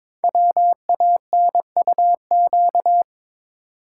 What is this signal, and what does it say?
Morse code, heard as a keyed tone, about 22 wpm, 700 Hz. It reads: WANUQ
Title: Code Groups - 5 Letter:Numbers - 22wpm